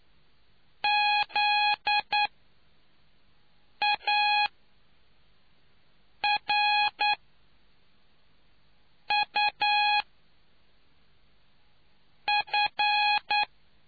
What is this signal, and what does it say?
Morse code, heard as a keyed tone, about 9 wpm, spaced as for 4 wpm, 830 Hz. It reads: ZARUF